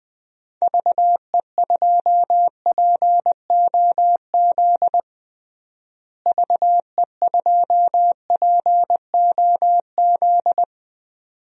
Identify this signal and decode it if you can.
Morse code, heard as a keyed tone, about 20 wpm, 700 Hz. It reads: VE2POZ VE2POZ
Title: DXCC-100 Call Signs - 6 Characters - Repeated 2x - 20wpm